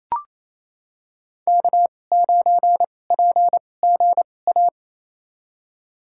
Morse code, heard as a keyed tone, about 28 wpm, spaced as for 20 wpm, 700 Hz. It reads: K9PGA
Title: USA Call Signs - 5 Characters - 2x Character Spacing - 28wpm